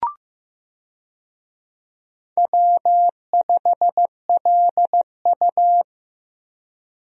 Morse code, heard as a keyed tone, about 15 wpm, 700 Hz. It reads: W5LU
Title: USA Call Signs - 4 Characters - Repeated 3x - 15wpm